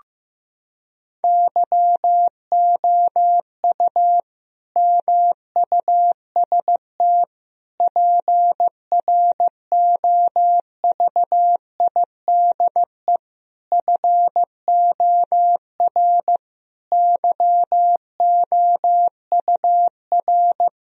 Morse code, heard as a keyed tone, about 15 wpm, 700 Hz. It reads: YOU MUST PROVIDE FOR YOUR
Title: Sentences from Top 1000 Words - Review Part 3 - 15wpm